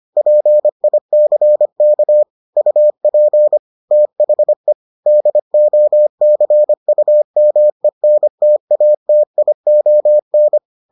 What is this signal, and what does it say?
Morse code, heard as a keyed tone, about 25 wpm, 600 Hz. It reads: PICK UP THE DOCUMENTATION